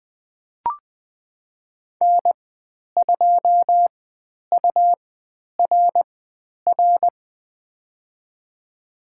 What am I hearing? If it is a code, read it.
Morse code, heard as a keyed tone, about 20 wpm, spaced as for 10 wpm, 700 Hz. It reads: N2URR
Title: 109 Lesson 18g - Letter m - Call Signs 20wpm